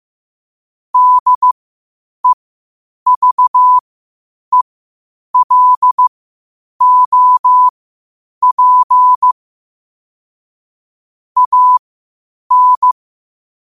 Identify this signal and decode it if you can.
Morse code, heard as a keyed tone, about 15 wpm, spaced as for 8 wpm, 1000 Hz. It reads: DEVELOP AN
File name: qotdfiles / 8wpm